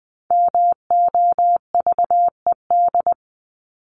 Morse code, heard as a keyed tone, about 20 wpm, 700 Hz. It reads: MOVED
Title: ED Suffix - Top 100 Words with Base Words 20wpm